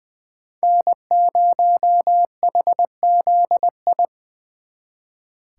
Morse code, heard as a keyed tone, about 25 wpm, 700 Hz. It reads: N0HZI
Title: Speed Racing - USA 1x3 Call Signs 20wpm